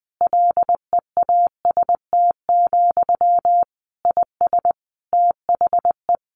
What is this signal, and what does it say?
Morse code, heard as a keyed tone, about 20 wpm, 700 Hz. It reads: LEAST, IS THE
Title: Sentences from Top 1500 Words 20wpm